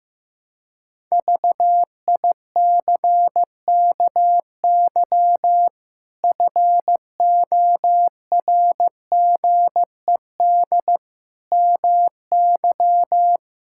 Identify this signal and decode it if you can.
Morse code, heard as a keyed tone, about 15 wpm, 700 Hz. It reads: VICKY FORGED MY
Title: Pangram Challenge v2 - 15wpm